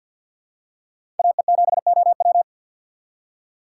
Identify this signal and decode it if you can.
Morse code, heard as a keyed tone, about 48 wpm, 700 Hz. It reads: AE7OW
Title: USA Call Signs - 5 Characters - 6x Character Spacing - 50wpm